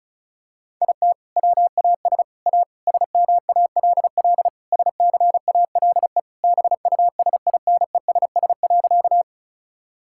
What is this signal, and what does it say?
Morse code, heard as a keyed tone, about 36 wpm, 700 Hz. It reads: IT WAS A SMALL SCALE BUSINESS.
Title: Sentences from Top 1000 Words - Review Part 5 - 35wpm